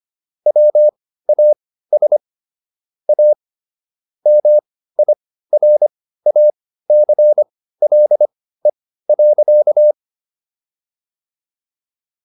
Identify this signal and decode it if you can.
Morse code, heard as a keyed tone, about 25 wpm, spaced as for 15 wpm, 600 Hz. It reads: WAS A MIRACLE.